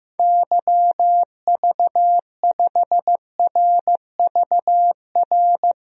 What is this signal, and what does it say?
Morse code, heard as a keyed tone, about 15 wpm, 700 Hz. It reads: YV5RVR
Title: DXCC-100 Call Signs - 6 Characters - 15wpm